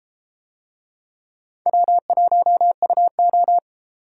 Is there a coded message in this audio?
Yes: Morse code from 1.7 to 3.6 s, heard as a keyed tone, about 33 wpm, 700 Hz. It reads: W1UO